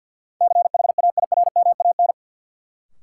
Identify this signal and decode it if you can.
Morse code, heard as a keyed tone, about 52 wpm, 700 Hz. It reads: CHAIRMAN
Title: Sets of 3 Words - New Word Drill - 50wpm